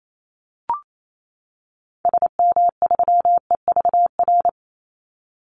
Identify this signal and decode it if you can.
Morse code, heard as a keyed tone, about 28 wpm, 700 Hz. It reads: SM3EVR
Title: Call Signs - 28wpm